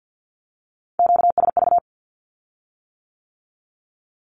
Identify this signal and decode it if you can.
Morse code, heard as a keyed tone, about 50 wpm, 700 Hz. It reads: QSV
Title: Q-Codes with Spoken Meaning - 50wpm